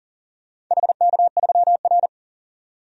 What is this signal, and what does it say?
Morse code, heard as a keyed tone, about 40 wpm, 700 Hz. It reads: HK3R
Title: DXCC-100 Call Signs - 4 Characters - 40wpm